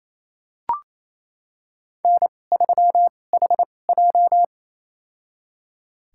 Morse code, heard as a keyed tone, about 28 wpm, spaced as for 20 wpm, 700 Hz. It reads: N3HJ
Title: USA Call Signs - 4 Characters - 2x Character Spacing - 28wpm